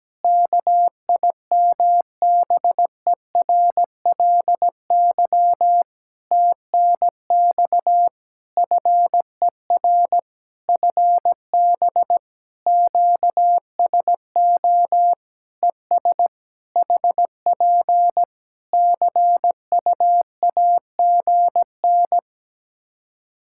Morse code, heard as a keyed tone, about 17 wpm, 700 Hz. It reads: KIMBERLY TNX FER FB QSO ES HP CUAGN
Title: Practice QSO - Exchange 4 - Ending - 17wpm